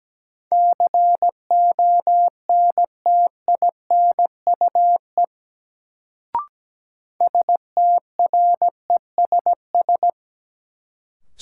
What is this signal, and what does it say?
Morse code, heard as a keyed tone, about 17 wpm, 700 Hz. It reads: CONTINUE STRESS
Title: Sentences from Top 1000 Words - Review Part 6 - New Word Drill - 17wpm